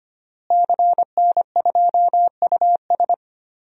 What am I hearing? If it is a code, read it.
Morse code, heard as a keyed tone, about 25 wpm, 700 Hz. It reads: CN2US
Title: DXCC-100 Call Signs - 5 Characters - Repeated 2x - 25wpm